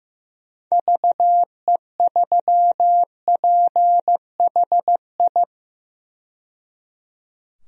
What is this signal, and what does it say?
Morse code, heard as a keyed tone, about 15 wpm, 700 Hz. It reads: VE3PHI
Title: DXCC-100 Call Signs - 6 Characters - 15wpm